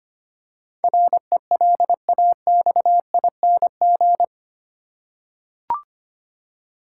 Morse code, heard as a keyed tone, about 25 wpm, 700 Hz. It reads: RELAXING
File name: 8 Letter Words - First 3 Letters Given - 25wpm